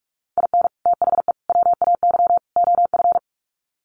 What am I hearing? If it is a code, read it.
Morse code, heard as a keyed tone, about 44 wpm, 700 Hz. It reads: IN THE WAY OF
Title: Sets of 4 Words 45wpm